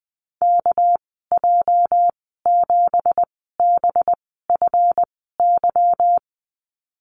Morse code, heard as a keyed tone, about 20 wpm, spaced as for 14 wpm, 700 Hz. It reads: KJ7BFY
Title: 213 Lesson 34g - Number 8 - Call Signs 20wpm